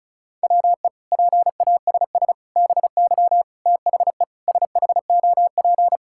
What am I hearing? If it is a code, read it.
Morse code, heard as a keyed tone, about 34 wpm, 700 Hz. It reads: WE PASS BY THE SHOP
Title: Sentences from Top 1600 Words 35wpm